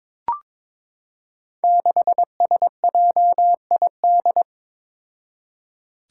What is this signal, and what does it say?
Morse code, heard as a keyed tone, about 21 wpm, 700 Hz. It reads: 6SJID
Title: Five Challenging Letter-Number Combinations 22wpm